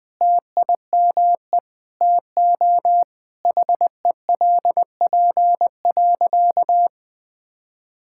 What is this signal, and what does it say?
Morse code, heard as a keyed tone, about 20 wpm, 700 Hz. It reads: TIME TO HELP.
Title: Sentences with 7-Word Phrases - Spoken Beginning - 20wpm